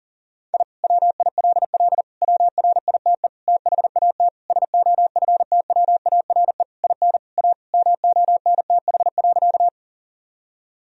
Morse code, heard as a keyed tone, about 37 wpm, 700 Hz. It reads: I WILL WRITE THAT SOFTWARE IN A MONTH.
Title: Sentences from Top 1000 Words - Review Part 3 - 40wpm